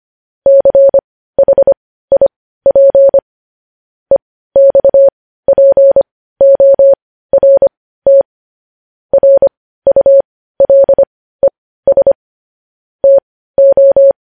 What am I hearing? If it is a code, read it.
Morse code, heard as a keyed tone, about 25 wpm, spaced as for 15 wpm, 550 Hz. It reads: CHIP EXPORT RULES TO
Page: News Headlines in Morse Code at 15 WPM